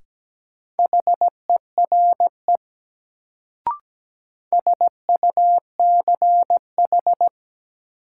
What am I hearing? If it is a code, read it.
Morse code, heard as a keyed tone, about 17 wpm, 700 Hz. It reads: HERE SUCH
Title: Sentences from Top 200 Words - New Word Drill - 17wpm